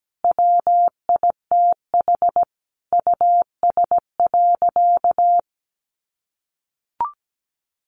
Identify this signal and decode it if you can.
Morse code, heard as a keyed tone, about 17 wpm, 700 Hz. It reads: WITH US.
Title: Sentences from Top 1200 Words 17wpm